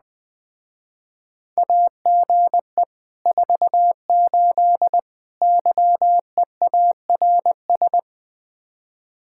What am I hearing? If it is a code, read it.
Morse code, heard as a keyed tone, about 20 wpm, 700 Hz. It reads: AGE 48 YEARS
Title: Speed-Racing Common QSO Phrases - 17wpm